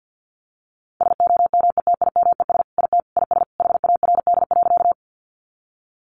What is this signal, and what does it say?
Morse code, heard as a keyed tone, about 48 wpm, 700 Hz. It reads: SOMETIMES IT IS HARD.
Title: Sentences from Top 600 Words 50wpm